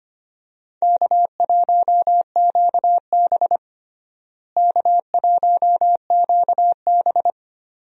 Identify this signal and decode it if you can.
Morse code, heard as a keyed tone, about 25 wpm, 700 Hz. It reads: K1QB K1QB
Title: USA Call Signs - 4 Characters - Repeated 2x - 25wpm